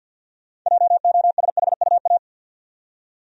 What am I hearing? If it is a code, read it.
Morse code, heard as a keyed tone, about 47 wpm, 700 Hz. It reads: JOSHUA
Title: USA Common Names - 50wpm